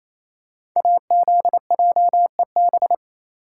Morse code, heard as a keyed tone, about 27 wpm, 700 Hz. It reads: AZJEB
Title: Code Groups - 5 Letter:Numbers - 28wpm